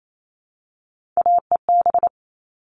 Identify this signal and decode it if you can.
Morse code, heard as a keyed tone, about 28 wpm, 700 Hz. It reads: AEB
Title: All Cut Numbers - 3-Digit 28wpm